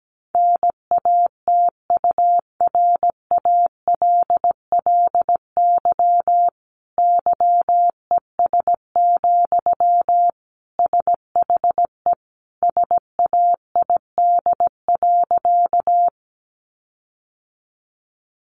Morse code, heard as a keyed tone, about 17 wpm, 700 Hz. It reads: NATURALLY YES, SHE SAID.
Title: Sentences from Top 1300 Words 17wpm